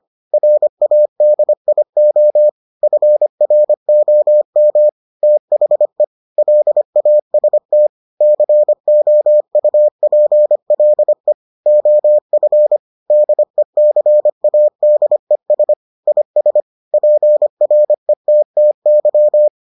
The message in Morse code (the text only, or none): RADIO FROM THE LAST COUPLE OF DECADES IS PRETTY